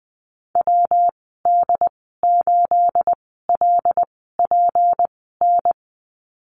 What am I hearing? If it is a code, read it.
Morse code, heard as a keyed tone, about 20 wpm, spaced as for 14 wpm, 700 Hz. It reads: WD8LPN
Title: 219 Lesson 35g - Number 0 - Call Signs 20wpm